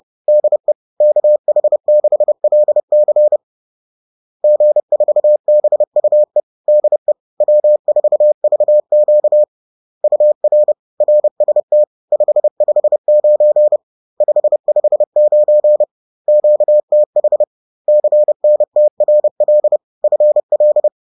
Morse code, heard as a keyed tone, about 30 wpm, 600 Hz. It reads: DE KH6LC G4BUE DE W4VQ UR RST 559 559 QTH CNTRL FL